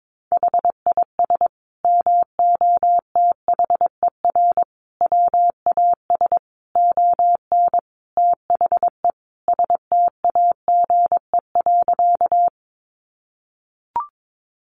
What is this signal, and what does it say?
Morse code, heard as a keyed tone, about 22 wpm, 700 Hz. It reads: HIS MOTHER WAS ON THE STAGE.
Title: Sentences from Top 1200 Words 22wpm